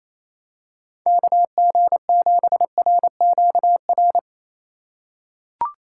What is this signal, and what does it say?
Morse code, heard as a keyed tone, about 28 wpm, 700 Hz. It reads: KG7RQR
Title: USA Call Signs - 4x Character Spacing - 28wpm